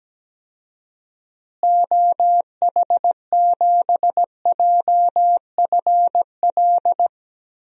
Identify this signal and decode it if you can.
Morse code, heard as a keyed tone, about 17 wpm, 700 Hz. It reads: OH7JFL
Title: DXCC-100 Call Signs - 6 Characters - Repeated 3x - 17wpm